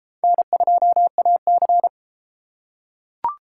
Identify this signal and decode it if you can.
Morse code, heard as a keyed tone, about 31 wpm, 700 Hz. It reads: N2AC